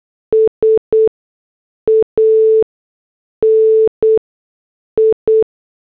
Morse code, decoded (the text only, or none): SANI